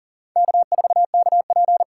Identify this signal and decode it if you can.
Morse code, heard as a keyed tone, about 40 wpm, 700 Hz. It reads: K4KP
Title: USA Call Signs - 4 Characters - 2x Character Spacing - 40wpm